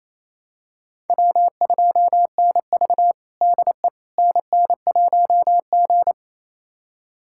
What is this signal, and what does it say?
Morse code, heard as a keyed tone, about 28 wpm, 700 Hz. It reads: W2NV DE NN1G
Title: USA Callsign DE Callsign - 4 Characters - 28wpm